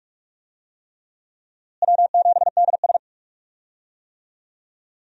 Morse code, heard as a keyed tone, about 47 wpm, 700 Hz. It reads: W7DS
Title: USA Call Signs - 4 Characters - Repeated 3x - 45wpm